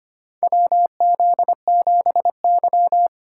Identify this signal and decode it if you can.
Morse code, heard as a keyed tone, about 25 wpm, 700 Hz. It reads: WZ7Y
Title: USA Call Signs - 4 Characters - 6x Character Spacing - 25wpm